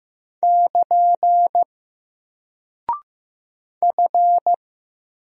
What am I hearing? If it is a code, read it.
Morse code, heard as a keyed tone, about 15 wpm, 700 Hz. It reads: <KN> F
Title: Single Character (Letter:Number:Punctuation:Prosign) Alt 15wpm